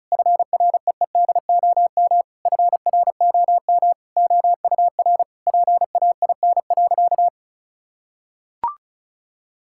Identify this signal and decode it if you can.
Morse code, heard as a keyed tone, about 34 wpm, 700 Hz. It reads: FREEDOM FROM OUR PAIN.
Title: Sentences from Top 1600 Words 35wpm